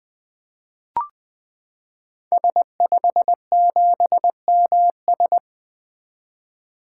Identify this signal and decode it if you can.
Morse code, heard as a keyed tone, about 20 wpm, 700 Hz. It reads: S57MS
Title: 2023 SOTA Activator Call Signs - 20wpm